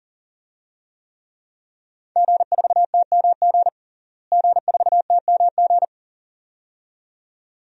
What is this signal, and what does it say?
Morse code, heard as a keyed tone, about 40 wpm, 700 Hz. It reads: G4TMG G4TMG
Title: DXCC-100 Call Signs - 5 Characters - Repeated 2x - 40wpm